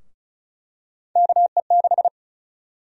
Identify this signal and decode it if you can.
Morse code, heard as a keyed tone, about 34 wpm, 700 Hz. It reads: KE6
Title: USA Call Sign Prefixes - Repeated 2x - 35wpm